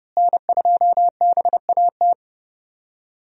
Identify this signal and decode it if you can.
Morse code, heard as a keyed tone, about 30 wpm, 700 Hz. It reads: N2BAT
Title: USA Call Signs - 5 Characters - Repeated 2x - 30wpm